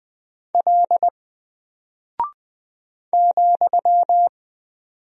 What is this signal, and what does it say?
Morse code, heard as a keyed tone, about 20 wpm, 700 Hz. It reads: L ,
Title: Single Character (Letter:Number:Punctuation:Prosign) 20wpm